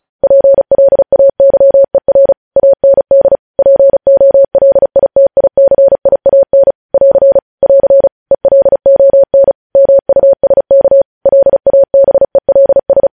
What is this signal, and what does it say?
Morse code, heard as a keyed tone, about 36 wpm, 550 Hz. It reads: PLAYER AND POLITICIAN <AR> <AR> ELON MUSK LABELS
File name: podcast / News headlines-1x-35x35